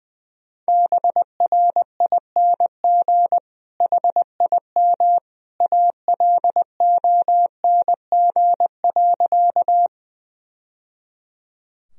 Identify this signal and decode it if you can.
Morse code, heard as a keyed tone, about 21 wpm, 700 Hz. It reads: BRING HIM ALONG.
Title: Sentences from Top 1000 Words - Review Part 3 - 20wpm